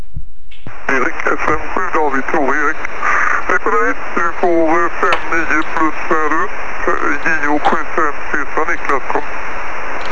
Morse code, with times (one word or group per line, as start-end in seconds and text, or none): none